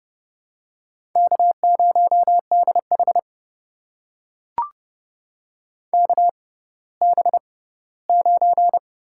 Morse code, heard as a keyed tone, about 30 wpm, spaced as for 17 wpm, 700 Hz. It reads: K0DH KB9
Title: USA Call Signs - 6x Character Spacing - 30wpm